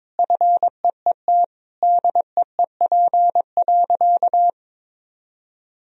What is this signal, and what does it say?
Morse code, heard as a keyed tone, about 22 wpm, 700 Hz. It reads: FEET DEEP.
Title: Sentences from Top 1000 Words - Review Part 3 - 22wpm